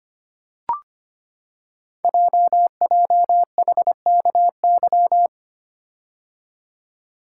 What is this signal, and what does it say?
Morse code, heard as a keyed tone, about 25 wpm, 700 Hz. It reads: JJHKY
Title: Code Groups - 5 Letters - 25wpm